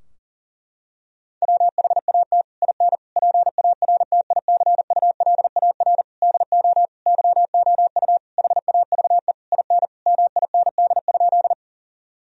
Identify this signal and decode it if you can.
Morse code, heard as a keyed tone, about 40 wpm, 700 Hz. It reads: WHAT IN PARTICULAR DO YOU HAVE IN MIND?
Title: Sentences from Top 1000 Words - Review Part 4 - 40wpm